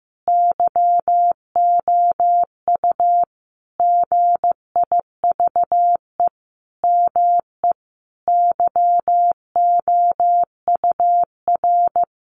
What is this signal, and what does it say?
Morse code, heard as a keyed tone, about 15 wpm, 700 Hz. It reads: YOU GIVE ME YOUR
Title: Sentences from Top 1200 Words 15wpm